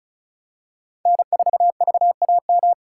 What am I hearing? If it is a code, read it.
Morse code, heard as a keyed tone, about 36 wpm, 700 Hz. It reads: N4VAM
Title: USA Callsign DE Callsign - 5 Characters - 35wpm